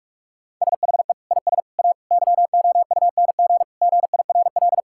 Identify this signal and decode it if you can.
Morse code, heard as a keyed tone, about 41 wpm, 700 Hz. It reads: SHE IS A YOUNG GIRL
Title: Sentences from Top 1600 Words 45wpm